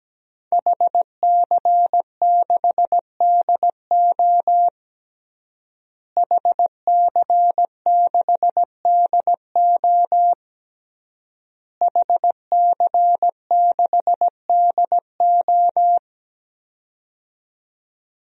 Morse code, heard as a keyed tone, about 17 wpm, 700 Hz. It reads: HC6DO HC6DO HC6DO
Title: DXCC-100 Call Signs - 5 Characters - Repeated 3x - 17wpm